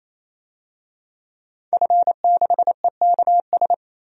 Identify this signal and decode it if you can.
Morse code, heard as a keyed tone, about 28 wpm, 700 Hz. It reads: F6EKS
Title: DXCC-100 Call Signs - 5 Characters - 28wpm